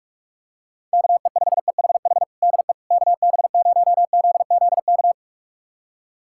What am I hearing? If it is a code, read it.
Morse code, heard as a keyed tone, about 46 wpm, 700 Hz. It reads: KE5EHH DE KB0ZZK
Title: USA Callsign DE Callsign - 6 Characters - 45wpm